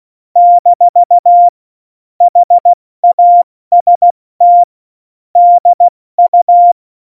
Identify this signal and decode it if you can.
Morse code, heard as a keyed tone, about 16 wpm, 700 Hz. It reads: - HAST DU
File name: Morsekurs / Witze / Witz 03. Oktober 17, Tempo 70 F80